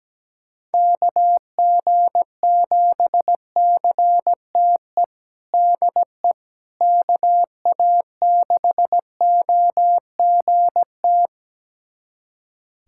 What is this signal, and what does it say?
Morse code, heard as a keyed tone, about 17 wpm, 700 Hz. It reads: KG7CTE DE KA6OGT
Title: USA Callsign DE Callsign - 6 Characters - 17wpm